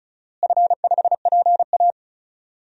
Morse code, heard as a keyed tone, about 36 wpm, 700 Hz. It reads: F5PA
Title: DXCC-100 Call Signs - 4 Characters - Repeated 2x - 35wpm